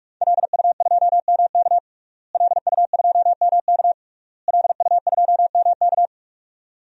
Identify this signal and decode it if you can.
Morse code, heard as a keyed tone, about 46 wpm, 700 Hz. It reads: LU2MK LU2MK LU2MK
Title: DXCC-100 Call Signs - 5 Characters - Repeated 3x - 45wpm